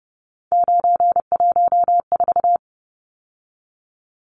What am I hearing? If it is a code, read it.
Morse code, heard as a keyed tone, about 30 wpm, 700 Hz. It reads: 914
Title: Numbers - 3-Digit 30wpm